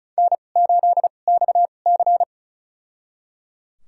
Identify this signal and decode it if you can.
Morse code, heard as a keyed tone, about 34 wpm, spaced as for 25 wpm, 700 Hz. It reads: N8XC
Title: USA Call Signs - 2x Character Spacing - 35wpm